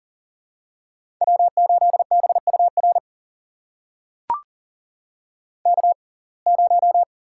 Morse code, heard as a keyed tone, about 40 wpm, spaced as for 29 wpm, 700 Hz. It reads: W8BUR K0
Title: USA Call Signs - 5 Characters - 6x Character Spacing - 40wpm